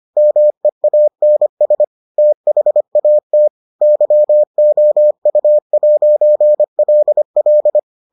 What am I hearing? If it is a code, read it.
Morse code, heard as a keyed tone, about 25 wpm, 600 Hz. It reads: MEANS THAT YOU'LL